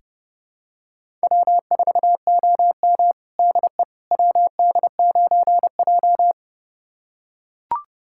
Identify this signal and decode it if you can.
Morse code, heard as a keyed tone, about 30 wpm, 700 Hz. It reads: W4OM DE WD9J
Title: USA Callsign DE Callsign - 4 Characters - 30wpm